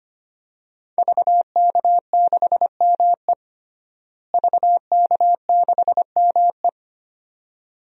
Morse code, heard as a keyed tone, about 25 wpm, 700 Hz. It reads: VK6ME VK6ME
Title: DXCC-100 Call Signs - 5 Characters - Repeated 2x - 25wpm